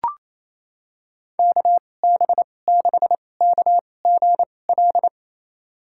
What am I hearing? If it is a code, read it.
Morse code, heard as a keyed tone, about 28 wpm, spaced as for 20 wpm, 700 Hz. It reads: KB6KGL
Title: USA Call Signs - 6 Characters - 2x Character Spacing - 28wpm